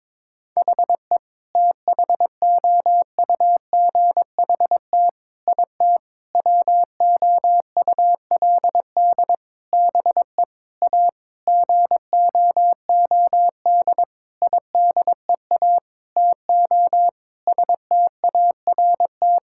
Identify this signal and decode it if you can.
Morse code, heard as a keyed tone, about 22 wpm, 700 Hz. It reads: HE THOUGHT IT WOULD BE A GOOD IDEA TO START
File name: practice / 22wpm / Sentences with 8-Word Phrases - 22wpm